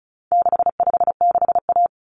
Morse code, heard as a keyed tone, about 34 wpm, 700 Hz. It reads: 656A
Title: Common Cut Numbers - 4-Digit 35wpm